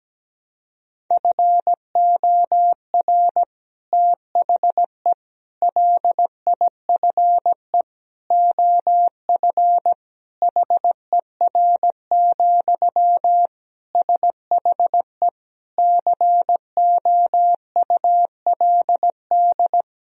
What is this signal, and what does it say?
Morse code, heard as a keyed tone, about 17 wpm, 700 Hz. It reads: FOR THE LIFE OF HER, SHE COULD
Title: Sentences from Top 1000 Words - Review Part 2 - 17wpm